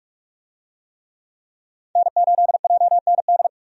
Morse code, heard as a keyed tone, about 44 wpm, 700 Hz. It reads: N8JND
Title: USA Call Signs - 5 Characters - 45wpm